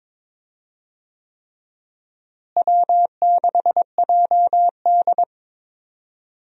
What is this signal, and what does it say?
Morse code, heard as a keyed tone, about 21 wpm, 700 Hz. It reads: W6JD
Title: USA Call Signs - 4 Characters - Repeated 3x - 22wpm